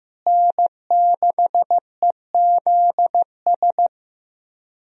Morse code, heard as a keyed tone, about 15 wpm, 700 Hz. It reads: N6EZS
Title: USA Call Signs - 5 Characters - Repeated 2x - 15wpm